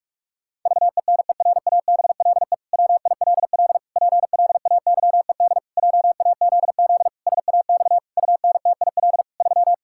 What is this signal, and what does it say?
Morse code, heard as a keyed tone, about 41 wpm, 700 Hz. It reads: VENERABLE WILL PLAYED JAZZ SAX UNTIL 3
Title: Pangram Challenge v2 - 45wpm